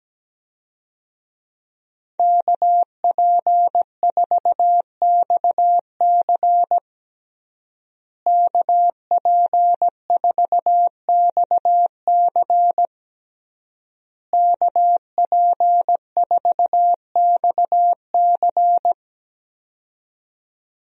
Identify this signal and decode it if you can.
Morse code, heard as a keyed tone, about 17 wpm, 700 Hz. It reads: KP4XC KP4XC KP4XC
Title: DXCC-100 Call Signs - 5 Characters - Repeated 3x - 17wpm